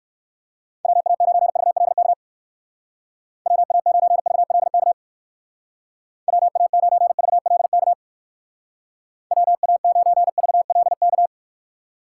Morse code, heard as a keyed tone, about 57 wpm, 700 Hz. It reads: WA9VLK WA9VLK WA9VLK WA9VLK